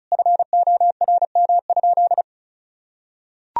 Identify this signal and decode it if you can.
Morse code, heard as a keyed tone, about 36 wpm, 700 Hz. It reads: FORM?
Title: Sentences from Top 1000 Words - Review Part 1 - 35wpm